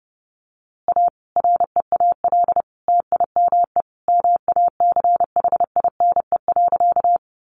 Morse code, heard as a keyed tone, about 30 wpm, 700 Hz. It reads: A REAL TIME MACHINE.
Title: A Sound Of Thunder 30wpm